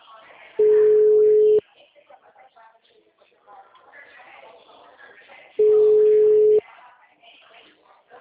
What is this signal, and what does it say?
Ringback tone (Europe, 425 Hz), with speech.